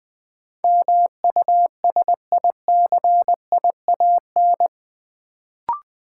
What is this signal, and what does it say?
Morse code, heard as a keyed tone, about 20 wpm, 700 Hz. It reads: MUSICIAN
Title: Build A Word - 3 Letters At A Time - 8-Letter Words - 20wpm